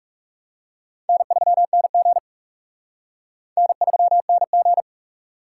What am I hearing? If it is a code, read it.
Morse code, heard as a keyed tone, about 38 wpm, 700 Hz. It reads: N3NG N3NG